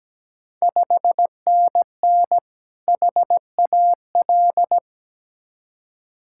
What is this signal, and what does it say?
Morse code, heard as a keyed tone, about 17 wpm, 700 Hz. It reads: 5NN HAL
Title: Ontario QSO Party Exchange 17wpm